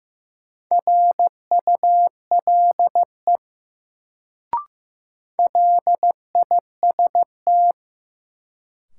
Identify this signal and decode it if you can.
Morse code, heard as a keyed tone, about 15 wpm, 700 Hz. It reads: RULE LIST
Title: Sentences from Top 400 Words - New Word Drill - 15wpm